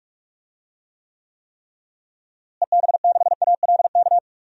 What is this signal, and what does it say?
Morse code, heard as a keyed tone, about 44 wpm, 700 Hz. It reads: EB6ALK
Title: DXCC-100 Call Signs - 6 Characters - Repeated 3x - 45wpm